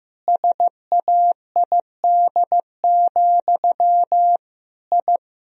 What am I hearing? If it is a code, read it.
Morse code, heard as a keyed tone, about 15 wpm, 700 Hz. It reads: SAID, I
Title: Sentences from Top 1000 Words - Review Part 1 - 15wpm